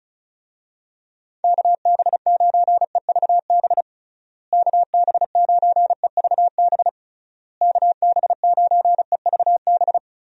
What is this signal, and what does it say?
Morse code, heard as a keyed tone, about 36 wpm, 700 Hz. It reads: KB9EVB KB9EVB KB9EVB
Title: USA Call Signs - 6 Characters - Repeated 3x - 35wpm